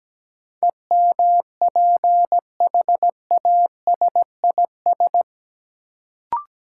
Morse code, heard as a keyed tone, about 17 wpm, 700 Hz. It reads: EMPHASIS
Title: Sentences from Top 1400 Words - New Word Drill - 17wpm